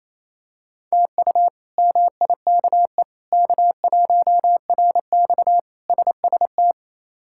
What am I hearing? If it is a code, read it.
Morse code, heard as a keyed tone, about 27 wpm, 700 Hz. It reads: TU MIKE K1RX SST
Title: K1USN SST QSO Practice - 28wpm